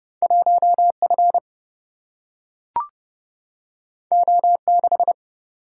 Morse code, heard as a keyed tone, about 30 wpm, 700 Hz. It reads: 1 F O 6
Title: Single-Letter Single-Number 30wpm